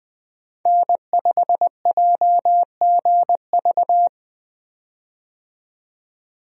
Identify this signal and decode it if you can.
Morse code, heard as a keyed tone, about 20 wpm, 700 Hz. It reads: N5JGV
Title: USA Call Signs - 5 Characters - 20wpm